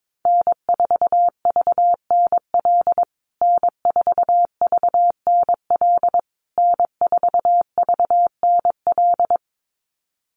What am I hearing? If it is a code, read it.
Morse code, heard as a keyed tone, about 22 wpm, 700 Hz. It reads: N4VNL N4VNL N4VNL
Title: USA Call Signs Repeated 3x 22wpm